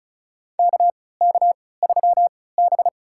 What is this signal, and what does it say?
Morse code, heard as a keyed tone, about 34 wpm, spaced as for 20 wpm, 700 Hz. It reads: KK3B
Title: USA Call Signs - 4 Characters - 3x Character Spacing - 35wpm